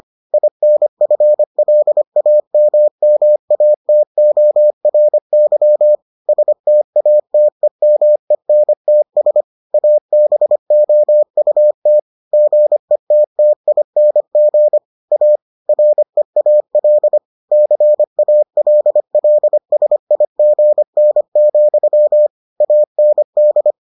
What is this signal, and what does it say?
Morse code, heard as a keyed tone, about 25 wpm, 600 Hz. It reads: INFLAMMATORY STATEMENTS ABOUT GETTING A REAL CALLSIGN, AND